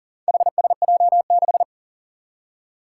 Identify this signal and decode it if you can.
Morse code, heard as a keyed tone, about 40 wpm, 700 Hz. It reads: HSJ6
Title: Four Challenging Letter-Number Combinations 40wpm